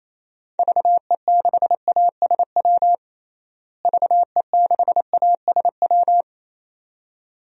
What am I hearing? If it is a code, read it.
Morse code, heard as a keyed tone, about 27 wpm, 700 Hz. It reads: VE6ASW VE6ASW
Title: DXCC-100 Call Signs - 6 Characters - Repeated 2x - 28wpm